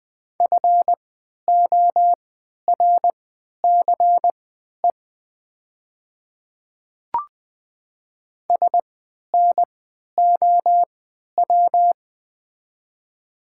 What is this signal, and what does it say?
Morse code, heard as a keyed tone, about 20 wpm, spaced as for 11 wpm, 700 Hz. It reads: FORCE SNOW
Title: 3x Character Spacing - Top 500 Words - 20wpm